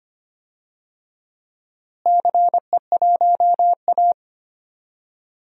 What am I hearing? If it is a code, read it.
Morse code, heard as a keyed tone, about 25 wpm, 700 Hz. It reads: CE1A
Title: DXCC-100 Call Signs - 4 Characters - Repeated 2x - 25wpm